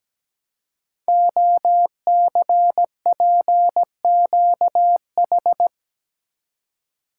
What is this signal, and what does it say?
Morse code, heard as a keyed tone, about 17 wpm, 700 Hz. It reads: OCPQH
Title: Code Groups - 5 Letter:Numbers - 17wpm